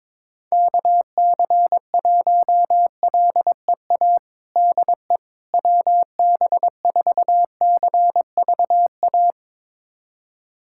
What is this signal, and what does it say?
Morse code, heard as a keyed tone, about 22 wpm, 700 Hz. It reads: KC1LEA DE WB4CVA
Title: USA Callsign DE Callsign 22wpm